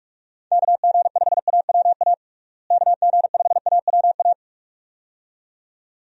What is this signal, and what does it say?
Morse code, heard as a keyed tone, about 46 wpm, 700 Hz. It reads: KG5AWA KG5AWA
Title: USA Call Signs - 6 Characters - Repeated 2x - 45wpm